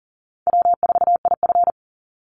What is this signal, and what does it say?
Morse code, heard as a keyed tone, about 40 wpm, 700 Hz. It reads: W4IF
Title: Call Signs - 40wpm